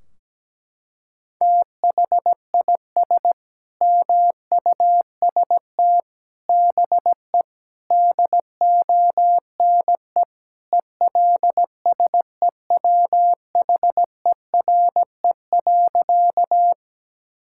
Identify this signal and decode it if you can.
Morse code, heard as a keyed tone, about 17 wpm, 700 Hz. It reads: THIS MUST BE DONE ELSEWHERE.
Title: Sentences from Top 1600 Words 17wpm